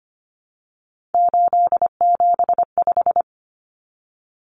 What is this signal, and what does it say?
Morse code, heard as a keyed tone, about 25 wpm, 700 Hz. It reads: 875
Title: Numbers - 3-Digit 25wpm